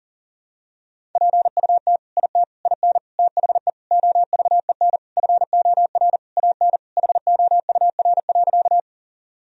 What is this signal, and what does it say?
Morse code, heard as a keyed tone, about 40 wpm, 700 Hz. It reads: PUT IT IN THE OVEN FOR AN HOUR.
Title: Sentences from Top 1000 Words - Review Part 4 - 40wpm